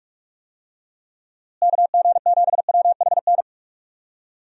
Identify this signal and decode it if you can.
Morse code, heard as a keyed tone, about 46 wpm, 700 Hz. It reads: KG7WHN
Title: USA Call Signs - 6 Characters - Repeated 2x - 45wpm